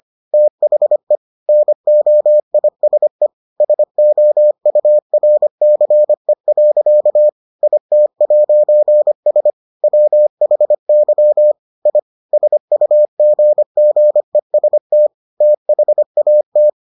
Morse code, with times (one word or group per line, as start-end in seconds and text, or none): 0.3-1.2 THE
1.5-3.3 NOISE
3.6-7.3 SOURCE.
7.6-9.5 IT'S
9.8-11.5 WHY
11.9-12.0 I
12.3-15.1 SUGGEST
15.4-16.7 THAT